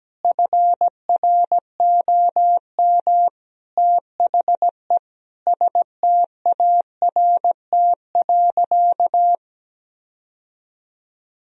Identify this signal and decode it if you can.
Morse code, heard as a keyed tone, about 17 wpm, 700 Hz. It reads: FROM THE START.
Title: Sentences from Top 1000 Words - Review Part 1 - 17wpm